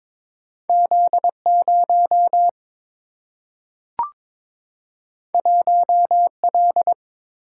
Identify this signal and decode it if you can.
Morse code, heard as a keyed tone, about 22 wpm, 700 Hz. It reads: Z0 1L
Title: Single-Letter Single-Number 22wpm